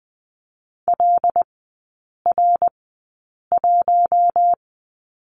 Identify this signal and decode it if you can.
Morse code, heard as a keyed tone, about 20 wpm, spaced as for 8 wpm, 700 Hz. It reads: LR1
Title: 074 Lesson 13e - Letter L - Three Characters 20wpm